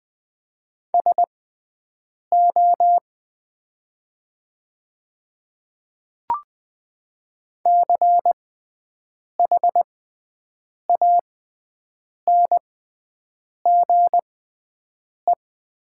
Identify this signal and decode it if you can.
Morse code, heard as a keyed tone, about 20 wpm, spaced as for 7 wpm, 700 Hz. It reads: SO CHANGE
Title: 6x Character Spacing - Top 200 Words - 20wpm